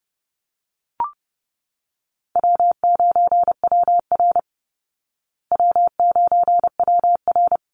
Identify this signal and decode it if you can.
Morse code, heard as a keyed tone, about 30 wpm, 700 Hz. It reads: W9WR W9WR
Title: Easy Call Signs Repeated 2x - 30wpm